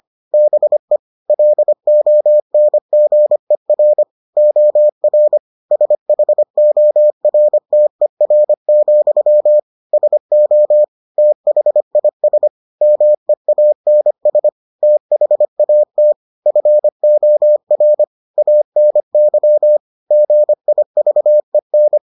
Morse code, heard as a keyed tone, about 25 wpm, 600 Hz. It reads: BE LONGER OR SHORTER, SO THIS MEANS THAT FOR ANY GIVEN